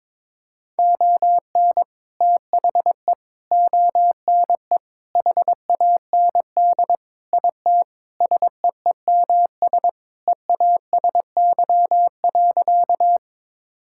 Morse code, heard as a keyed tone, about 22 wpm, 700 Hz. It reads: ON THE ONE HAND IT SEEMS EASY.
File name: practice / 22wpm / Sentences with 4-Word Phrases - 22wpm